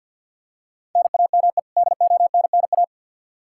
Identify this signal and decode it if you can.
Morse code, heard as a keyed tone, about 48 wpm, 700 Hz. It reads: NAME DONNA
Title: Practice QSO - Exchange 1 - Beginning - 50wpm